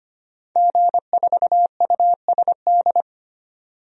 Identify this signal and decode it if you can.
Morse code, heard as a keyed tone, about 25 wpm, 690 Hz. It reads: G4USD
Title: DXCC-100 Call Signs - 5 Characters - 25wpm